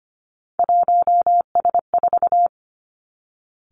Morse code, heard as a keyed tone, about 25 wpm, 700 Hz. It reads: 1S4
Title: Three Challenging Letter-Number Combinations 25wpm